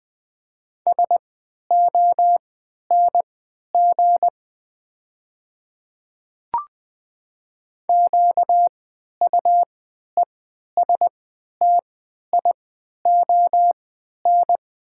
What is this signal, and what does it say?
Morse code, heard as a keyed tone, about 20 wpm, spaced as for 11 wpm, 700 Hz. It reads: SONG QUESTION
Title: 3x Character Spacing - Top 500 Words - 20wpm